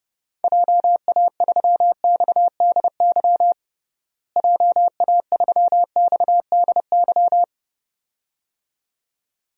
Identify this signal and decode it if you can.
Morse code, heard as a keyed tone, about 30 wpm, 700 Hz. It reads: JA3XDY JA3XDY
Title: DXCC-100 Call Signs - 6 Characters - Repeated 2x - 30wpm